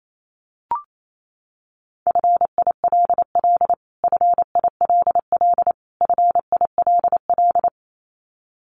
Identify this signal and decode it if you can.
Morse code, heard as a keyed tone, about 28 wpm, 700 Hz. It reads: FILL FILL FILL
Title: Top 500 Words Repeated 28wpm